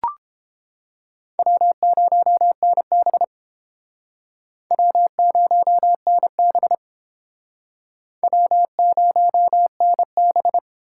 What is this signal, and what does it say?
Morse code, heard as a keyed tone, about 30 wpm, 700 Hz. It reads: W0NB W0NB W0NB